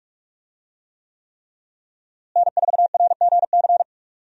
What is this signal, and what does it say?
Morse code, heard as a keyed tone, about 46 wpm, 700 Hz. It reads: N4RGC
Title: USA Call Signs - 5 Characters - Repeated 3x - 45wpm